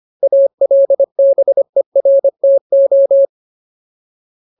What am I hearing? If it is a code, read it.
Morse code, heard as a keyed tone, about 25 wpm, 550 Hz. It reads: ALBERTO
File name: Ultimate Word List / 25WPM Ultimate